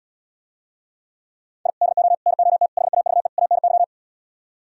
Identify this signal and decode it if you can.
Morse code, heard as a keyed tone, about 77 wpm, 690 Hz. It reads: I DO NOT HAVE ANY